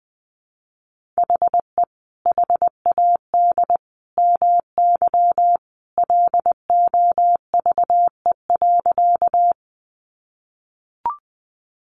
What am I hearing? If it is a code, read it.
Morse code, heard as a keyed tone, about 20 wpm, 700 Hz. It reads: HE HAD MY LOVE.
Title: Sentences from Top 400 Words 20wpm